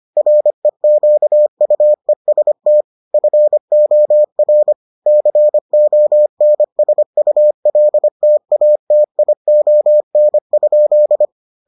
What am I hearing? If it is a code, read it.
Morse code, heard as a keyed tone, about 25 wpm, 600 Hz. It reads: REQUEST FOR CONSULTATION?